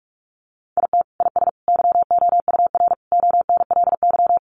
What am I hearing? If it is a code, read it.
Morse code, heard as a keyed tone, about 44 wpm, 700 Hz. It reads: IT IS YOUR ONLY
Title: Sentences from Top 800 Words 45wpm